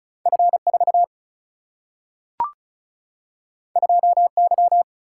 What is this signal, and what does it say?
Morse code, heard as a keyed tone, about 36 wpm, 700 Hz. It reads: F 4 2 Y
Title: Single-Letter Single-Number 35wpm